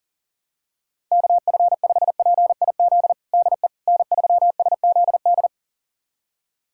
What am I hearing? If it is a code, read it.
Morse code, heard as a keyed tone, about 40 wpm, 700 Hz. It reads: KF5PIZ DE N3SZD
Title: USA Callsign DE Callsign 40wpm